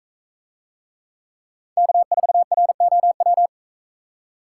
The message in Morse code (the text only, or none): K4ROW